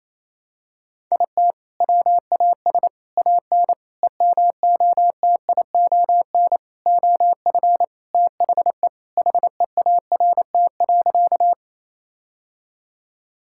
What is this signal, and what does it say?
Morse code, heard as a keyed tone, about 28 wpm, 700 Hz. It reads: IT WAS AN EMOTION OF THE HEART.
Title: Sentences from Top 1600 Words 28wpm